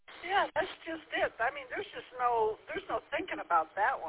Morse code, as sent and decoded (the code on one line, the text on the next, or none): none